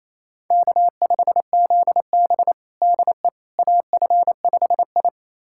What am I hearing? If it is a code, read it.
Morse code, heard as a keyed tone, about 28 wpm, 700 Hz. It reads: K5ZB DE AF5I
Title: USA Callsign DE Callsign - 4 Characters - 28wpm